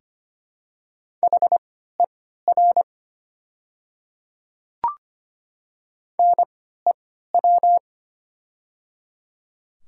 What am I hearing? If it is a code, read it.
Morse code, heard as a keyed tone, about 25 wpm, spaced as for 14 wpm, 700 Hz. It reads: HER NEW
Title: 3x Character Spacing - Top 200 Words - 25wpm